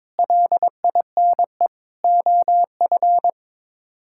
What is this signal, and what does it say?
Morse code, heard as a keyed tone, about 22 wpm, 700 Hz. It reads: LINE OF
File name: practice / 22wpm / Sets of 8 Words - 22wpm